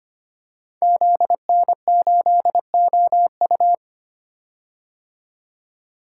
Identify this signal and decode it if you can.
Morse code, heard as a keyed tone, about 25 wpm, 700 Hz. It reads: ZN8OU
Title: Code Groups - 5 Letter:Numbers - 25wpm